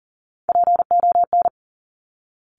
Morse code, heard as a keyed tone, about 39 wpm, 700 Hz. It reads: PON
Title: Top Three Letter Combinations in Words 40wpm